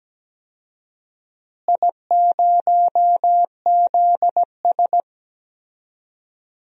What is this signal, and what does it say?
Morse code, heard as a keyed tone, about 17 wpm, 700 Hz. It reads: I0ZS
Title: DXCC-100 Call Signs - 4 Characters - Repeated 3x - 17wpm